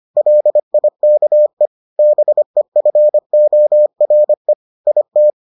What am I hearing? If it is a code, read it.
Morse code, heard as a keyed tone, about 25 wpm, 600 Hz. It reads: LIKE BEFORE IT